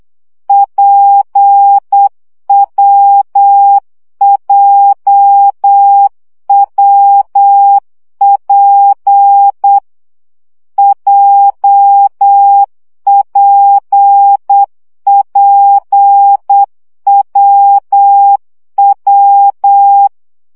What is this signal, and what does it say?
Morse code, heard as a keyed tone, about 8 wpm, 800 Hz. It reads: PWJWP JPPWW
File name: CW gyakorlas w p j 7 baud